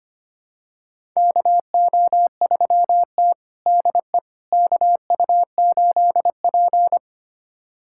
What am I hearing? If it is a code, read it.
Morse code, heard as a keyed tone, about 25 wpm, 700 Hz. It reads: KO3T DE KU8P
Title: USA Callsign DE Callsign - 4 Characters - 25wpm